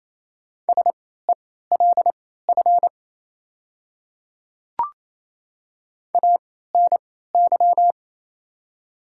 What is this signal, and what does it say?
Morse code, heard as a keyed tone, about 27 wpm, spaced as for 16 wpm, 700 Hz. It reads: SELF ANY
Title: 3x Character Spacing - Top 200 Words - 28wpm